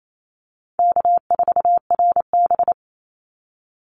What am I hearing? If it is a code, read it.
Morse code, heard as a keyed tone, about 28 wpm, 700 Hz. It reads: K4RB
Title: Easy Call Signs Repeated 2x - 28wpm